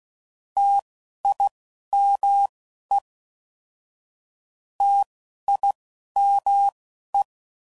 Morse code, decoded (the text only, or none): TIME TIME